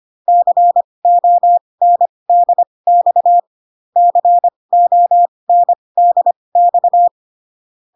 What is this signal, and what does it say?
Morse code, heard as a keyed tone, about 25 wpm, spaced as for 20 wpm, 700 Hz. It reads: CONDX CONDX